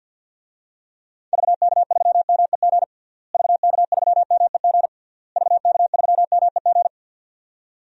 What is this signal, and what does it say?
Morse code, heard as a keyed tone, about 52 wpm, 700 Hz. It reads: VK3MEG VK3MEG VK3MEG
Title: DXCC-100 Call Signs - 6 Characters - Repeated 3x - 50wpm